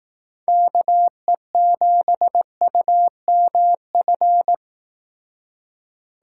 Morse code, heard as a keyed tone, about 18 wpm, 700 Hz. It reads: KE7UMF
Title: Speed Racing - USA 2x3 Call Signs 15wpm